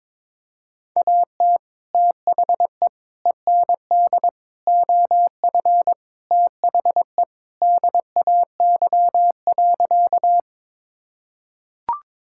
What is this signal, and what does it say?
Morse code, heard as a keyed tone, about 22 wpm, 700 Hz. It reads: AT THE END OF THE DAY.
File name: practice / 22wpm / Sentences with 2-Word Phrases - 22wpm